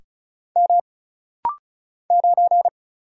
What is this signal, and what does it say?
Morse code, heard as a keyed tone, about 34 wpm, 700 Hz. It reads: M 9
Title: Single Letter-Number Alt - Rapid-Fire 35wpm